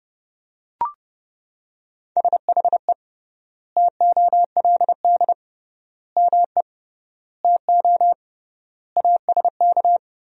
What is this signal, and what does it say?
Morse code, heard as a keyed tone, about 30 wpm, 700 Hz. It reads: SHE TOLD ME TO ASK
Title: 3x Word-Spacing - Sentences from Top 400 Words - 30wpm